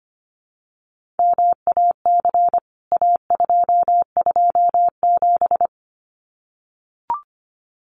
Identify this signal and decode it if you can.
Morse code, heard as a keyed tone, about 25 wpm, 700 Hz. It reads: MAC A227
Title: CWT Exchange 25wpm